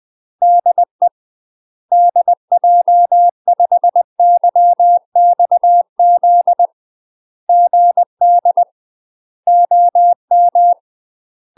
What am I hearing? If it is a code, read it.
Morse code, heard as a keyed tone, about 20 wpm, 700 Hz. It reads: DE DJ5YXZ GD OM
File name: UK MASTER QSOs TRAINING / 20WPM / UK MASTER QSOs 0063